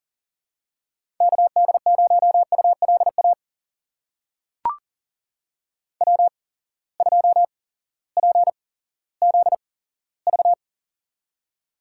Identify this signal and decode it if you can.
Morse code, heard as a keyed tone, about 38 wpm, spaced as for 18 wpm, 700 Hz. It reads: KD0ULA W2PZV
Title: USA Call Signs - 8x Character Spacing - 40wpm